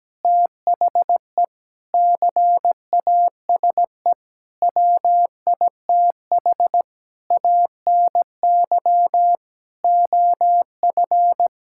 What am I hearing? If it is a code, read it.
Morse code, heard as a keyed tone, about 17 wpm, 700 Hz. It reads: THE CASE WITH ANY OF
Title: Sentences from Top 1600 Words 17wpm